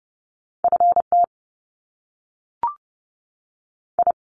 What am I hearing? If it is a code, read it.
Morse code, heard as a keyed tone, about 30 wpm, 700 Hz. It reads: F T I
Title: Top Two Letter Combinations in Words 30wpm